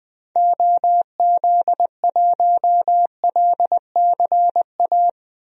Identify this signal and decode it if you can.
Morse code, heard as a keyed tone, about 20 wpm, 700 Hz. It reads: OZ1LCA
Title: DXCC-100 Call Signs - 6 Characters - Repeated 2x - 20wpm